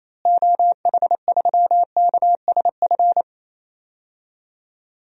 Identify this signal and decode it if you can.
Morse code, heard as a keyed tone, about 28 wpm, 700 Hz. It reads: OH3KSF
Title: DXCC-100 Call Signs - 6 Characters - Repeated 2x - 28wpm